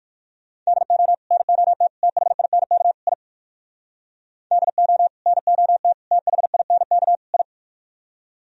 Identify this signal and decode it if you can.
Morse code, heard as a keyed tone, about 62 wpm, 700 Hz. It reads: DO NOT THINK I DO NOT THINK I